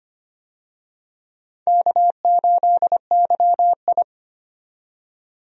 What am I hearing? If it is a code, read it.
Morse code, heard as a keyed tone, about 25 wpm, 700 Hz. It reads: K8YI
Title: USA Call Signs - 4 Characters - 25wpm